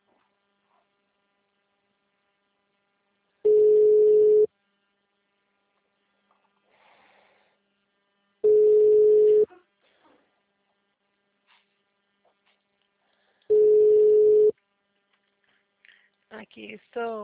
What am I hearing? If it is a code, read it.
Ringback tone (Europe, 425 Hz), with speech.
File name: out- -2049- - - .3002